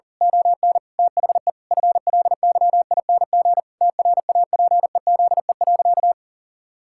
Morse code, heard as a keyed tone, about 40 wpm, 700 Hz. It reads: ON THE FLYING TRAPEZE.
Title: Pangram Challenge v2 - 40wpm